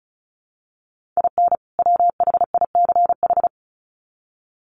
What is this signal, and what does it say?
Morse code, heard as a keyed tone, about 34 wpm, 700 Hz. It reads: IN WHICH
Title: Sets of 2 Words 35wpm